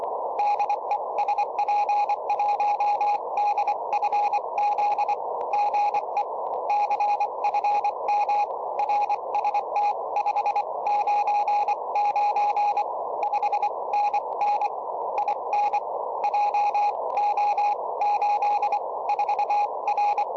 Morse code, heard as a keyed tone, about 25 wpm, spaced as for 21 wpm, 830 Hz, with radio noise. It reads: DE SP1DFZ GE CFM RST599 5NN IN JO84R
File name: tropo / sp1dfz